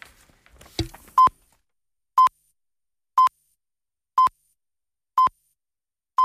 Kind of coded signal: radio time signal